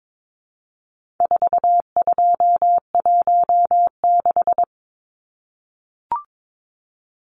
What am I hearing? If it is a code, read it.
Morse code, heard as a keyed tone, about 22 wpm, 700 Hz. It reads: 4216
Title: Numbers - 4-Digit 22wpm